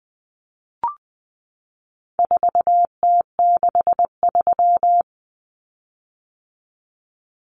Morse code, heard as a keyed tone, about 20 wpm, 700 Hz. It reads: 4T63
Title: Common Cut Numbers - 4-Digit 20wpm